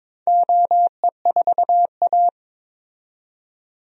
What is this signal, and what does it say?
Morse code, heard as a keyed tone, about 22 wpm, 700 Hz. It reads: OE4A
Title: DXCC-100 Call Signs - 4 Characters - 22wpm